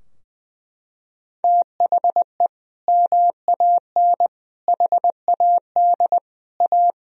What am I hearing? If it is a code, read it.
Morse code, heard as a keyed tone, about 20 wpm, 700 Hz. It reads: THE MAN HAD A
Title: Sentences from Top 1000 Words - Review Part 2 - 20wpm